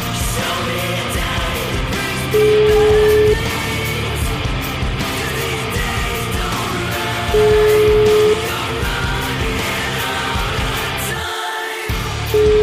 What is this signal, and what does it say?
Ringback tone (Europe, 425 Hz), with noise.